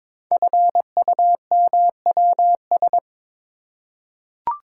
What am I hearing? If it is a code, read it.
Morse code, heard as a keyed tone, about 22 wpm, 700 Hz. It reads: FUMWS
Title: Code Groups - 5 Letters - 22wpm